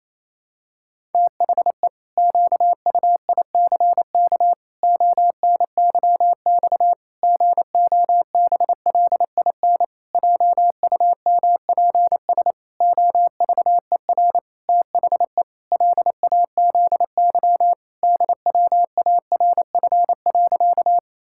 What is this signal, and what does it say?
Morse code, heard as a keyed tone, about 28 wpm, 700 Hz. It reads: THE QUICK ONYX GOBLIN JUMPS OVER THE LAZY DWARF.
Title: Pangram Challenge v2 - 28wpm